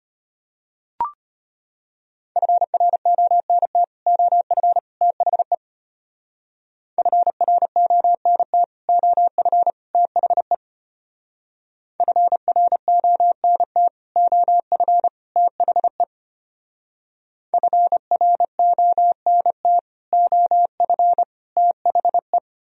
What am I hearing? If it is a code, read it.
Morse code, heard as a keyed tone, about 33 wpm, 700 Hz. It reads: FRONT OF THE FRONT OF THE FRONT OF THE FRONT OF THE